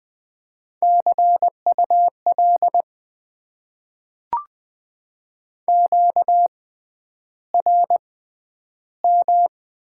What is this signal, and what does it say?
Morse code, heard as a keyed tone, about 20 wpm, spaced as for 10 wpm, 700 Hz. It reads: CUL QRM
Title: Top 90 QSO Elements - 6x Character Spacing - 20wpm